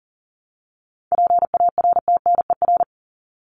Morse code, heard as a keyed tone, about 40 wpm, 700 Hz. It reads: PARTNER
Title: Top 1501-1750 Words 40wpm